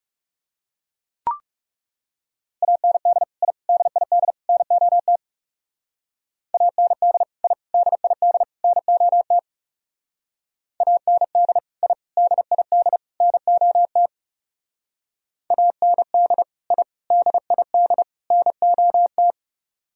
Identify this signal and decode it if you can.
Morse code, heard as a keyed tone, about 37 wpm, 700 Hz. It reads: AND I DID NOT AND I DID NOT AND I DID NOT AND I DID NOT
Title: Speed-Racing - Sets of 4 Words 30wpm